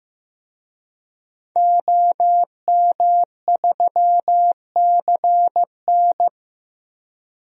Morse code, heard as a keyed tone, about 15 wpm, 700 Hz. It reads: OM3CN
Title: DXCC-100 Call Signs - 5 Characters - 15wpm